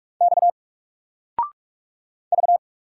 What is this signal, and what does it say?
Morse code, heard as a keyed tone, about 44 wpm, 700 Hz. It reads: X V
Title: Single Character (Letter:Number:Punctuation:Prosign) 45wpm